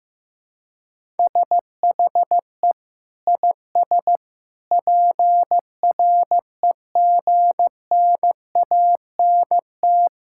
Morse code, heard as a keyed tone, about 15 wpm, 700 Hz. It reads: SHE IS PREGNANT
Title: Sentences from Top 1000 Words - Review Part 6 - 15wpm